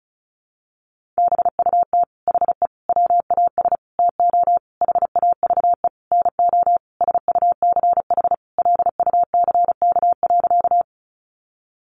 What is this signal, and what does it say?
Morse code, heard as a keyed tone, about 34 wpm, 700 Hz. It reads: BUT HE WAS TO HAVE NO SUCH LUCK.
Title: Sentences from Top 1500 Words 35wpm